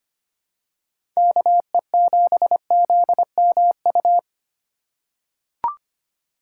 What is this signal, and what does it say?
Morse code, heard as a keyed tone, about 25 wpm, 700 Hz. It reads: KE7ZMU
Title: USA Call Signs - 6 Characters - 4x Character Spacing - 25wpm